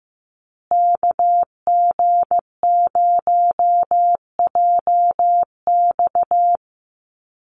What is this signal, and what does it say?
Morse code, heard as a keyed tone, about 15 wpm, 700 Hz. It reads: KG0JX
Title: USA Call Signs Repeated 3x 15wpm